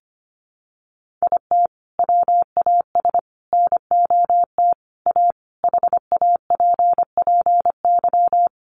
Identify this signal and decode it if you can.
Morse code, heard as a keyed tone, about 25 wpm, 700 Hz. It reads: IT WAS NOT A HAPPY
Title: Sentences from Top 1000 Words 25wpm